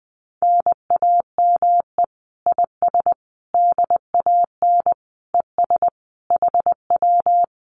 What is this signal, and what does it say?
Morse code, heard as a keyed tone, about 20 wpm, 700 Hz. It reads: NAME IS DAN ES HW
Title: Common QSO Phrases Encore 20wpm